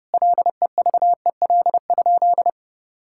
Morse code, heard as a keyed tone, about 32 wpm, 700 Hz. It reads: LEVEL?
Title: Sentences from Top 1000 Words - Review Part 5 - 30wpm